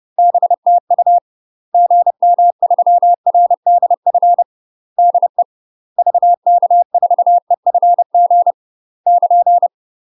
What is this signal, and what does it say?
Morse code, heard as a keyed tone, about 30 wpm, 700 Hz. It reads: BTU GM3RDF DE VK4EFG <KN>
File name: UK MASTER QSOs TRAINING / 30WPM / UK MASTER QSOs 0020